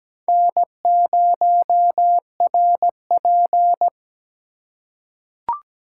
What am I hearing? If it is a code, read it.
Morse code, heard as a keyed tone, about 17 wpm, 700 Hz. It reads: N0RP
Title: USA Call Signs - 2x Character Spacing - 17wpm